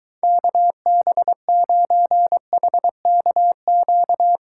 Morse code, heard as a keyed tone, about 23 wpm, 700 Hz. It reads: KB9HKQ